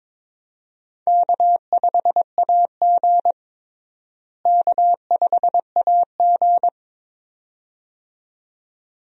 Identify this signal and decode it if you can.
Morse code, heard as a keyed tone, about 22 wpm, 700 Hz. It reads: K5AG K5AG
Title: USA Call Signs - 4 Characters - Repeated 2x - 22wpm